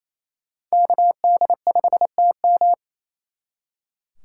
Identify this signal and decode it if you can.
Morse code, heard as a keyed tone, about 36 wpm, 700 Hz. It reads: KD5TM